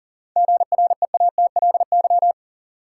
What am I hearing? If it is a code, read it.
Morse code, heard as a keyed tone, about 40 wpm, 700 Hz. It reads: GREATLY
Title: Sentences from Top 1400 Words - New Word Drill - 40wpm